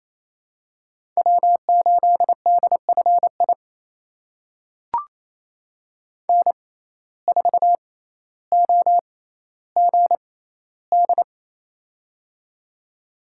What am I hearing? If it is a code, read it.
Morse code, heard as a keyed tone, about 28 wpm, spaced as for 14 wpm, 700 Hz. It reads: W8DFI N4OGD
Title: USA Call Signs - 5 Characters - 6x Character Spacing - 28wpm